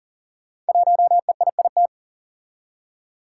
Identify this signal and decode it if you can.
Morse code, heard as a keyed tone, about 43 wpm, 700 Hz. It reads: 1EIIT
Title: Five Challenging Letter-Number Combinations 40wpm